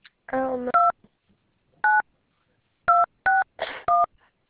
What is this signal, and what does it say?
DTMF, heard as telephone touch tones, tones of about 180 ms. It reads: 29261
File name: Info / by sbrown@ask @ 3 18 10 AM